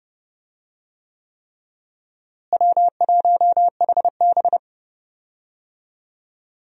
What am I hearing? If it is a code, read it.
Morse code, heard as a keyed tone, about 32 wpm, 700 Hz. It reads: W1HB
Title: USA Call Signs - 4 Characters - Repeated 3x - 30wpm